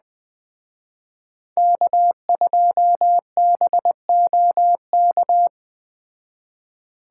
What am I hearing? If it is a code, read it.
Morse code, heard as a keyed tone, about 20 wpm, 700 Hz. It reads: K2BOK
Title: Speed Racing - USA 1x3 Call Signs 17wpm